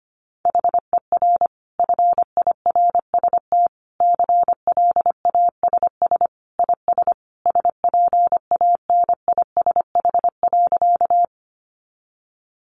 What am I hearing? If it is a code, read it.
Morse code, heard as a keyed tone, about 25 wpm, 700 Hz. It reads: HER FIRST CLASS IS SPANISH.
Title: Additional Easy Phrases 25wpm